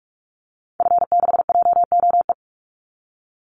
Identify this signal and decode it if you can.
Morse code, heard as a keyed tone, about 44 wpm, 700 Hz. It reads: F6JOE
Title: Call Signs - 45wpm